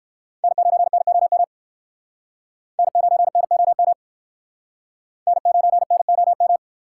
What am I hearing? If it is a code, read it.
Morse code, heard as a keyed tone, about 55 wpm, 700 Hz. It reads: N9NOM N9NOM N9NOM
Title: Speed Racing - USA 1x3 Call Signs 45wpm